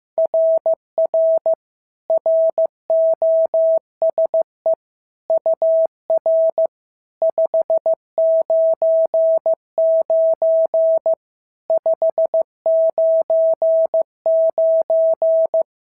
Morse code, heard as a keyed tone, about 14 wpm, 650 Hz. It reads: RR ROSE UR 599 599
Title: SOTA QSO Practice 15wpm